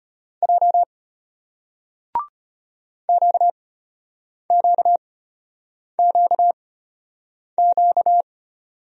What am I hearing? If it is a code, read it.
Morse code, heard as a keyed tone, about 36 wpm, spaced as for 8 wpm, 700 Hz. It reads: J QQQQ